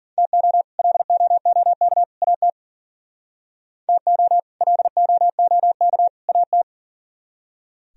Morse code, heard as a keyed tone, about 51 wpm, 700 Hz. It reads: TO LOOK AT TO LOOK AT